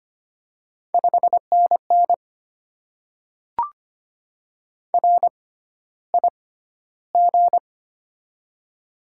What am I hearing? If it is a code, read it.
Morse code, heard as a keyed tone, about 25 wpm, spaced as for 13 wpm, 700 Hz. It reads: 5NN RIG
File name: Top 20 QSO Elements - 6x Character Spacing - 25wpm